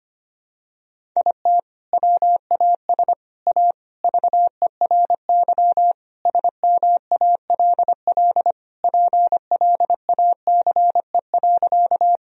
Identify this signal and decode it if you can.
Morse code, heard as a keyed tone, about 25 wpm, 700 Hz. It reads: IT WAS A VERY SMALL PLACE.
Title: Sentences from Top 1000 Words - Review Part 1 - 25wpm